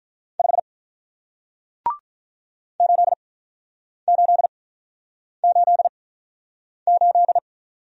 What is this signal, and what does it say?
Morse code, heard as a keyed tone, about 47 wpm, 700 Hz. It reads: 5 8 8 8 8